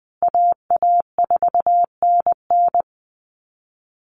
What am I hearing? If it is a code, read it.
Morse code, heard as a keyed tone, about 20 wpm, 700 Hz. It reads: AA4NN
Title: Call Signs Repeated 3x - 20wpm